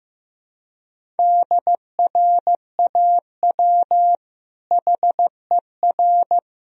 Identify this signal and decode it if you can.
Morse code, heard as a keyed tone, about 15 wpm, 700 Hz. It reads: DRAW HER
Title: Sentences from Top 1000 Words - Review Part 1 - 15wpm